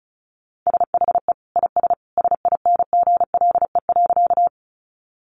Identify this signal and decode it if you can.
Morse code, heard as a keyed tone, about 34 wpm, 700 Hz. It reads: SHE IS SINGLE.
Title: Sentences from Top 700 Words 35wpm